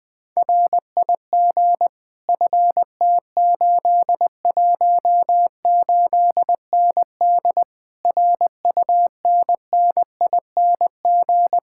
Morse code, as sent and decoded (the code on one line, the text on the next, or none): .-. .. --. / ..-. - ---.. .---- ---.. -. -.. / .-. ..- -. -. .. -. --.
RIG FT818ND RUNNING